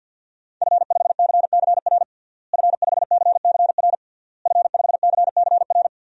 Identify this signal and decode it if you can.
Morse code, heard as a keyed tone, about 53 wpm, 700 Hz. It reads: F5CCR F5CCR F5CCR
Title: DXCC-100 Call Signs - 5 Characters - Repeated 3x - 50wpm